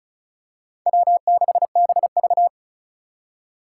Morse code, heard as a keyed tone, about 36 wpm, 700 Hz. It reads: W6BV
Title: Four Challenging Letter-Number Combinations 35wpm